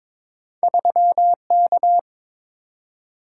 Morse code, heard as a keyed tone, about 22 wpm, 690 Hz. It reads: 3K
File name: practice / 22wpm / Two Letter-Number Combinations - 22wpm